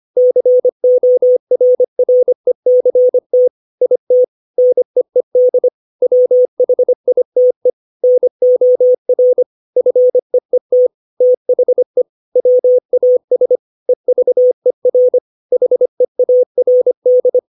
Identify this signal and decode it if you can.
Morse code, heard as a keyed tone, about 25 wpm, 500 Hz. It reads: CORRECT IT NEED WHITE NOR FEET THE WAS EVER HEARD
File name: 632 words / 632 Words 25WPM